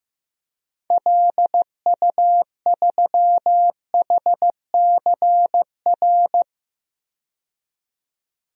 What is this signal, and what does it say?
Morse code, heard as a keyed tone, about 15 wpm, 700 Hz. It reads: LU3HCR
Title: DXCC-100 Call Signs - 6 Characters - 15wpm